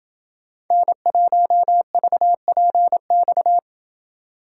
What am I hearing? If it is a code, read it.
Morse code, heard as a keyed tone, about 27 wpm, 700 Hz. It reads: N1VPX